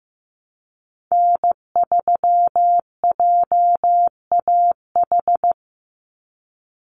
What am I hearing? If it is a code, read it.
Morse code, heard as a keyed tone, about 15 wpm, 700 Hz. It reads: N3JAH
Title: USA Call Signs 15wpm